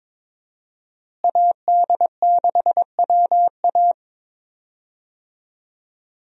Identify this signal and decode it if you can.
Morse code, heard as a keyed tone, about 22 wpm, 700 Hz. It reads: AD6WA
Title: USA Call Signs - 5 Characters - Repeated 3x - 22wpm